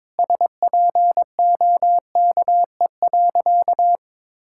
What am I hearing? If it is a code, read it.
Morse code, heard as a keyed tone, about 22 wpm, 700 Hz. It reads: SPOKE.
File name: practice / 22wpm / Sentences with 3-Word Phrases - 22wpm